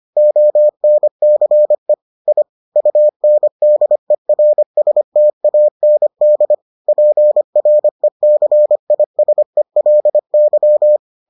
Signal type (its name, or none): Morse code